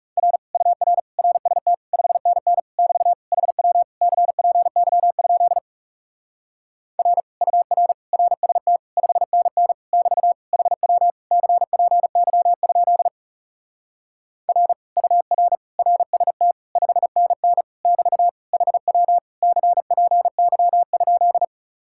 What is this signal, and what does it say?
Morse code, heard as a keyed tone, about 40 wpm, 700 Hz. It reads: R UR RST 5NN <BT> HW CPY? R UR RST 5NN <BT> HW CPY? R UR RST 5NN <BT> HW CPY?